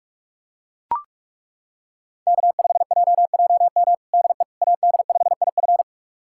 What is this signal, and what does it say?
Morse code, heard as a keyed tone, about 41 wpm, 700 Hz. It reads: K5JJM DE AD5IF
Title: USA Callsign DE Callsign - 5 Characters - 45wpm